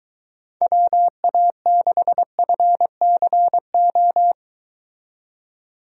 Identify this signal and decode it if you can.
Morse code, heard as a keyed tone, about 22 wpm, 700 Hz. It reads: WA6FCO